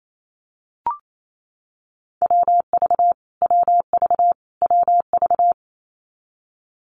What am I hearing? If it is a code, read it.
Morse code, heard as a keyed tone, about 28 wpm, 700 Hz. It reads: WV WV WV
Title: US States Repeated 3x 28wpm